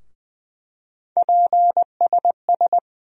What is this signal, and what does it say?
Morse code, heard as a keyed tone, about 20 wpm, 700 Hz. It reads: PSS
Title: USA Call Sign Postfixes - 3 Letters 20wpm